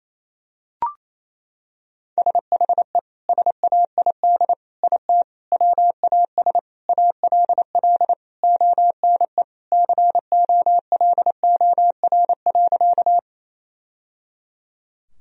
Morse code, heard as a keyed tone, about 28 wpm, 700 Hz. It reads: SHE SAID IT WAS ALL ONE COLOR.
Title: Sentences from Top 1000 Words - Review Part 2 - 28wpm